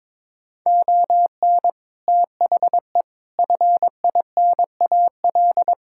Morse code, heard as a keyed tone, about 22 wpm, 700 Hz. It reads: ON THE FINAL
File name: practice / 22wpm / Sets of 7 Words - 22wpm